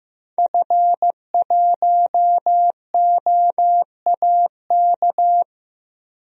Morse code, heard as a keyed tone, about 15 wpm, 700 Hz. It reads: F1OAK
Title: DXCC-100 Call Signs - 5 Characters - Repeated 3x - 15wpm